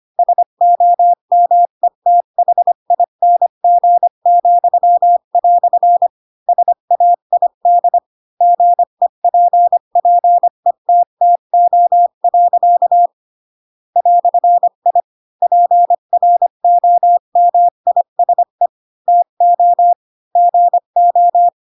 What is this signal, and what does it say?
Morse code, heard as a keyed tone, about 25 wpm, 700 Hz. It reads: SOMETHING," SAID GEPPETTO. "I PROMISE TO GO
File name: Pinocchio / Pinocchio ch8 20 25 ik1hge